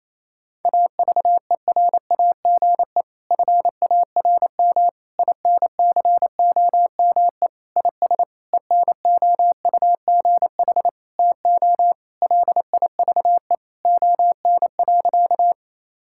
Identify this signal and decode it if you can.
Morse code, heard as a keyed tone, about 27 wpm, 700 Hz. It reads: AVERAGE FARM INCOME IS ENOUGH TO LIVE ON.
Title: Sentences from Top 1000 Words - Review Part 6 - 28wpm